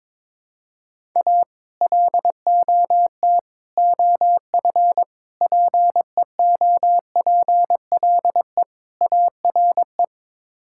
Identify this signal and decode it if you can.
Morse code, heard as a keyed tone, about 21 wpm, 700 Hz. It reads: A LOT OF PEOPLE ARE
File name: practice / 22wpm / Build A Phrase - 5-Word Phrases - 22wpm